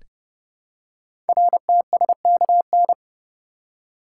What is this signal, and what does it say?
Morse code, heard as a keyed tone, about 30 wpm, 700 Hz. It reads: RTSKN
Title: Code Groups - 5 Letters - 30wpm